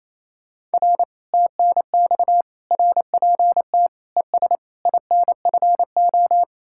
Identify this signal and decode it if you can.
Morse code, heard as a keyed tone, about 35 wpm, 700 Hz. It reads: R TNX RPT ES INFO
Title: Speed-Racing Common QSO Phrases Encore - 28wpm